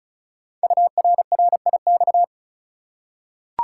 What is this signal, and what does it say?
Morse code, heard as a keyed tone, about 36 wpm, 700 Hz. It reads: URRIX
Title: Code Groups - 5 Letters - 35wpm